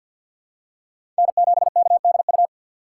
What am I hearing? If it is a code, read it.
Morse code, heard as a keyed tone, about 52 wpm, 700 Hz. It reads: N7KDU
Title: USA Call Signs - 5 Characters - 4x Character Spacing - 50wpm